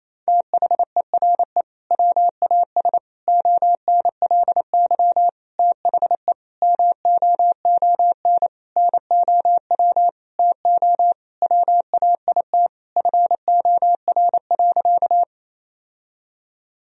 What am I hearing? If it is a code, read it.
Morse code, heard as a keyed tone, about 28 wpm, 700 Hz. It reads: THERE WAS ONLY THE MOON NOW TO WAIT FOR.
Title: Sentences from Top 1000 Words - Review Part 3 - 28wpm